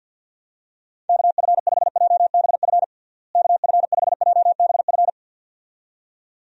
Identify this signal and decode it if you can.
Morse code, heard as a keyed tone, about 53 wpm, 700 Hz. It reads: KF5JBF KF5JBF
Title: USA Call Signs - 6 Characters - Repeated 2x - 50wpm